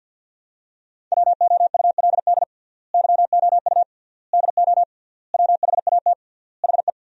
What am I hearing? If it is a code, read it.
Morse code, heard as a keyed tone, about 47 wpm, 700 Hz. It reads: WOULD YOU DO WHAT HE
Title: 3x Word-Spacing - Sentences from Top 100 Words - 50wpm